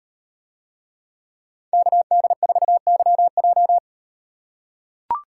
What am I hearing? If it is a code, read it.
Morse code, heard as a keyed tone, about 38 wpm, 700 Hz. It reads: KD4YJ